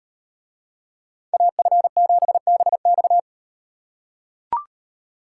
Morse code, heard as a keyed tone, about 37 wpm, 700 Hz. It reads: AF7BX